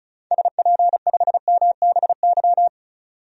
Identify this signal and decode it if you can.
Morse code, heard as a keyed tone, about 36 wpm, 700 Hz. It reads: SP5MBY
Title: DXCC-100 Call Signs - 6 Characters - 35wpm